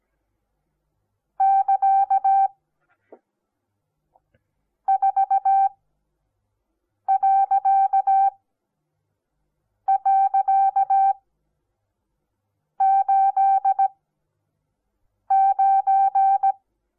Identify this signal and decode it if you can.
Morse code, heard as a keyed tone, about 17 wpm, spaced as for 5 wpm, 780 Hz. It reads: <KA> 4..89
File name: Onlinelehrgang / morsen / DJ4UF Morsen08